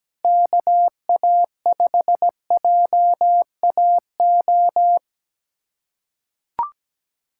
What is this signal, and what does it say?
Morse code, heard as a keyed tone, about 17 wpm, 700 Hz. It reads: KA5JAO
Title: USA Callsign DE Callsign - 6 Characters - 17wpm